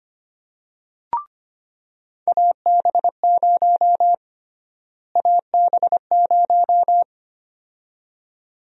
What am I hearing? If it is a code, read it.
Morse code, heard as a keyed tone, about 25 wpm, 700 Hz. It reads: AB0 AB0
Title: USA Call Sign Prefixes - Repeated 2x - 25wpm